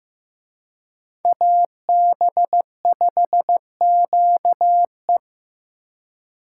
Morse code, heard as a keyed tone, about 18 wpm, 700 Hz. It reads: AB5QE
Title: Speed Racing - USA 2x2 Call Signs 15wpm